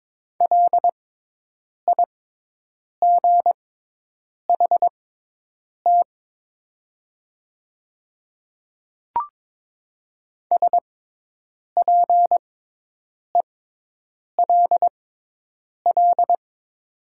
Morse code, heard as a keyed tone, about 22 wpm, spaced as for 8 wpm, 700 Hz. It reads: LIGHT SPELL
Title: 6x Character Spacing - Top 300 Words - 22wpm